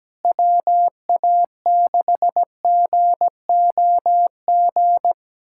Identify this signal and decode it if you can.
Morse code, heard as a keyed tone, about 17 wpm, 700 Hz. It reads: WA6GOG
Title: USA Callsign DE Callsign 17wpm